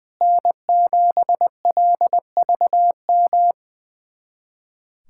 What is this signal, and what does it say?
Morse code, heard as a keyed tone, about 20 wpm, 700 Hz. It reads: N7LVM
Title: USA Callsign DE Callsign - 5 Characters - 20wpm